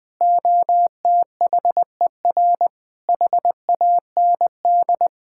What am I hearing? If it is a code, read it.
Morse code, heard as a keyed tone, about 20 wpm, 700 Hz. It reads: OTHER HAND
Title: Sentences with 3-Word Phrases - 20wpm